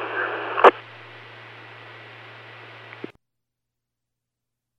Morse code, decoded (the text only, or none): none